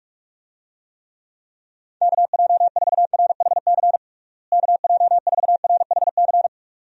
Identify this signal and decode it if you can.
Morse code, heard as a keyed tone, about 46 wpm, 700 Hz. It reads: KJ4RHC KJ4RHC
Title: USA Call Signs - 6 Characters - Repeated 2x - 45wpm